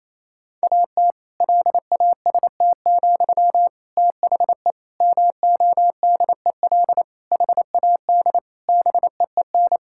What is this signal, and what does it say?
Morse code, heard as a keyed tone, about 28 wpm, 700 Hz. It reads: AT LAST, THE MODEL HAD BEEN
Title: Sentences from Top 1000 Words - Review Part 5 - 28wpm